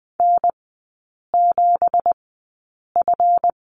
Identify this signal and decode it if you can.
Morse code, heard as a keyed tone, about 20 wpm, spaced as for 8 wpm, 700 Hz. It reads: N7F
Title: 166 Lesson 27e - Number 7 - Three Characters 20wpm